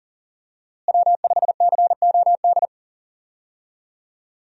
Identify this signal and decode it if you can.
Morse code, heard as a keyed tone, about 38 wpm, 700 Hz. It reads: W5COD
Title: USA Call Signs - 5 Characters - Repeated 3x - 40wpm